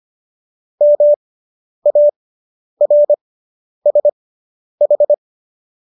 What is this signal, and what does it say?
Morse code, heard as a keyed tone, about 25 wpm, spaced as for 10 wpm, 590 Hz. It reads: MARSH